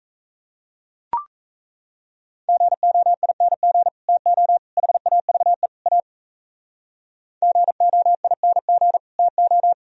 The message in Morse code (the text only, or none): GOING TO HAVE A GOING TO